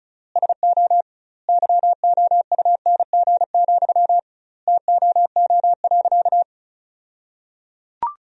Morse code, heard as a keyed tone, about 34 wpm, 700 Hz. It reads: SO YOUNG, TOO.
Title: 2x Word-Spacing - Sentences from Top 400 Words - 35wpm